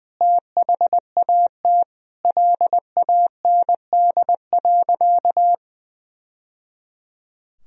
Morse code, heard as a keyed tone, about 21 wpm, 700 Hz. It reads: THAT LAND.
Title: Sentences from Top 1000 Words - Review Part 1 - 20wpm